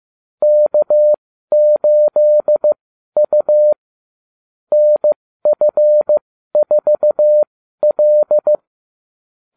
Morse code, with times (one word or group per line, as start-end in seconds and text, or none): 0.4-3.7 K8U
4.7-8.6 NF4L